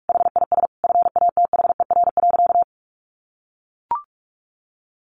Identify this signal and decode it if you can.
Morse code, heard as a keyed tone, about 44 wpm, 700 Hz. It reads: HIS FATHER.
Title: Sentences from Top 500 Words 45wpm